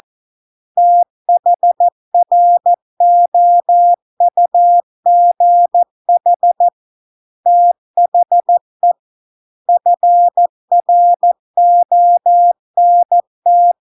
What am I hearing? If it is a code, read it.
Morse code, heard as a keyed tone, about 14 wpm, 700 Hz. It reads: THROUGH THE FRONT